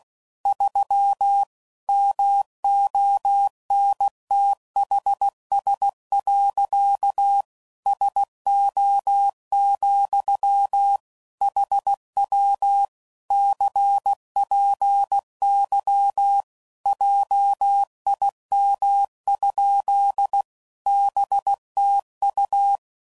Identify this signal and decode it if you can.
Morse code, heard as a keyed tone, about 16 wpm, 800 Hz. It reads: 3 MONTHS. SO, HW CPY JIM? BTU
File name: CW Code Groups @15WPM / QSOs / WD0BGP Spaced